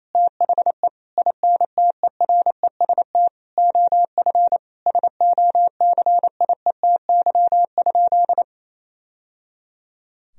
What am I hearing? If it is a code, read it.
Morse code, heard as a keyed tone, about 28 wpm, 700 Hz. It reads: THE INTEREST OF SOCIETY?
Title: Sentences from Top 1000 Words - Review Part 4 - 28wpm